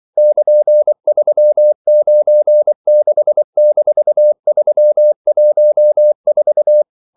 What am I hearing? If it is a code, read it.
Morse code, heard as a keyed tone, about 24 wpm, 600 Hz. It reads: <KN>396-314
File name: CW / Practice / EnglishQuotations0224